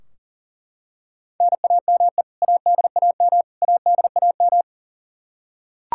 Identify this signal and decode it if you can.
Morse code, heard as a keyed tone, about 40 wpm, 700 Hz. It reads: NAME ADAM ADAM
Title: K1USN SST QSO Practice 40wpm